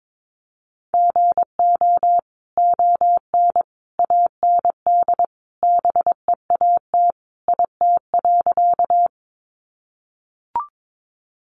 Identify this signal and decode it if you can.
Morse code, heard as a keyed tone, about 22 wpm, 700 Hz. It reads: GO ON AND BEAT IT.
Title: Sentences from Top 1300 Words 22wpm